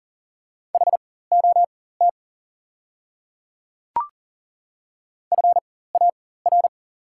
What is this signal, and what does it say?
Morse code, heard as a keyed tone, about 39 wpm, spaced as for 19 wpm, 700 Hz. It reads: HOT FAR
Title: 4x Character Spacing - Top 300 Words - 40wpm